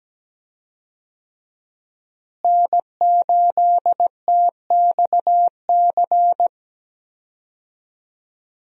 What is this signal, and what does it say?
Morse code, heard as a keyed tone, about 17 wpm, 700 Hz. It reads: N8TXC
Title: USA Call Signs - 5 Characters - Repeated 2x - 17wpm